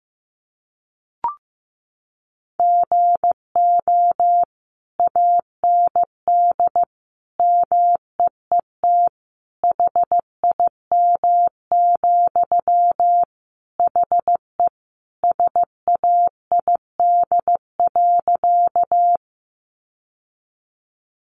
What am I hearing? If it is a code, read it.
Morse code, heard as a keyed tone, about 15 wpm, 700 Hz. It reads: GO AND MEET HIM, HE SAID.
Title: Sentences from Top 1100 Words 15wpm